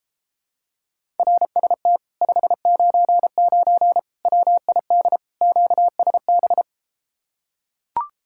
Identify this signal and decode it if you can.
Morse code, heard as a keyed tone, about 33 wpm, 700 Hz. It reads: RST 599 WID QSB